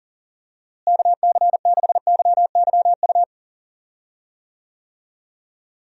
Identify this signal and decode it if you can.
Morse code, heard as a keyed tone, about 37 wpm, 700 Hz. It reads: KC6YYU
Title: USA Call Signs - 6 Characters - Repeated 2x - 40wpm